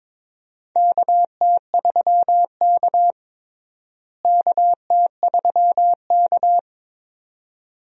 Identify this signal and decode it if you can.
Morse code, heard as a keyed tone, about 21 wpm, 700 Hz. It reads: KT3K KT3K
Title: USA Call Signs - 4 Characters - Repeated 2x - 22wpm